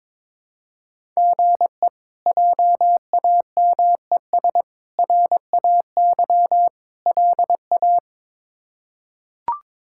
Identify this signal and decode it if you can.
Morse code, heard as a keyed tone, about 22 wpm, 700 Hz. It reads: GE JAMES RAY LA
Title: K1USN SST QSO Practice - 22wpm